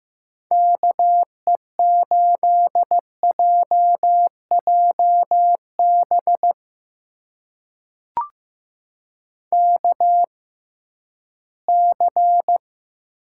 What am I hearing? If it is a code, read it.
Morse code, heard as a keyed tone, about 15 wpm, spaced as for 11 wpm, 700 Hz. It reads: KE8JJB KC
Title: USA Call Signs - 6 Characters - 6x Character Spacing - 15wpm